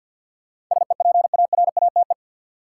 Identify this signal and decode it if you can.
Morse code, heard as a keyed tone, about 52 wpm, 700 Hz. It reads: SEPARATE
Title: Sentences from Top 1000 Words - Review Part 7 - New Word Drill - 50wpm